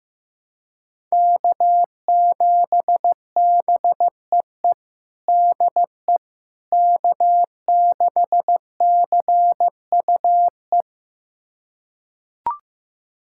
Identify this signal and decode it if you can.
Morse code, heard as a keyed tone, about 15 wpm, 700 Hz. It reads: K7BEE DE K6CUE
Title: USA Callsign DE Callsign - 5 Characters - 15wpm